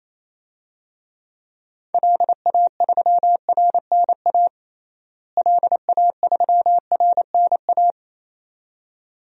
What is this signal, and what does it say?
Morse code, heard as a keyed tone, about 28 wpm, 700 Hz. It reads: LA3RNA LA3RNA
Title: DXCC-100 Call Signs - 6 Characters - Repeated 2x - 28wpm